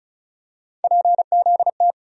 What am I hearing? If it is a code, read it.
Morse code, heard as a keyed tone, about 36 wpm, 700 Hz. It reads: PZT
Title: USA Call Sign Postfixes - 3 Letters 35wpm